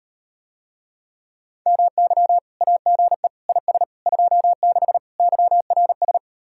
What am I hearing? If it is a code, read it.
Morse code, heard as a keyed tone, about 38 wpm, 700 Hz. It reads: MY AGE IS 26 YRS